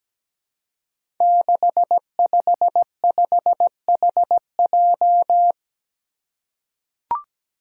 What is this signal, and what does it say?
Morse code, heard as a keyed tone, about 17 wpm, 700 Hz. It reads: 655HJ
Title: Five Challenging Letter-Number Combinations 17wpm